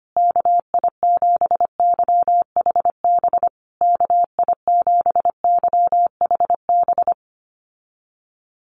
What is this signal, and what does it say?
Morse code, heard as a keyed tone, about 25 wpm, 700 Hz. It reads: KI7YHB KI7YHB
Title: USA Call Signs Repeated 3x 25wpm